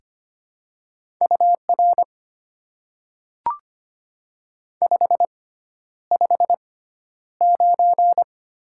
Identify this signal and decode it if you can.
Morse code, heard as a keyed tone, about 25 wpm, spaced as for 11 wpm, 690 Hz. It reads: UR 559
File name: Top 90 QSO Elements - 6x Character Spacing - 25wpm